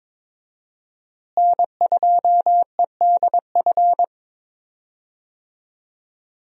Morse code, heard as a keyed tone, about 22 wpm, 700 Hz. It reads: N2EDF
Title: USA Call Signs - 5 Characters - Repeated 3x - 22wpm